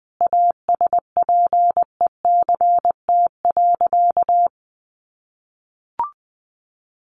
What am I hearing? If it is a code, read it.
Morse code, heard as a keyed tone, about 20 wpm, 700 Hz. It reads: ASPECT.
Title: Sentences from Top 1300 Words 20wpm